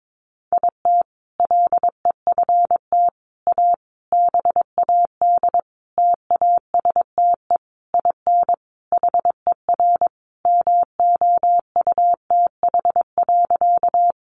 Morse code, heard as a keyed tone, about 22 wpm, 700 Hz. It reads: IT LEFT A BAD TASTE IN HER MOUTH.
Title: Sentences from Top 1500 Words 22wpm